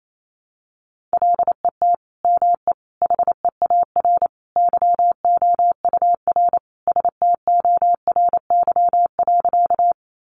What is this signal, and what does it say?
Morse code, heard as a keyed tone, about 28 wpm, 700 Hz. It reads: LET ME HEAR YOUR STORY.
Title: Sentences from Top 300 Words 28wpm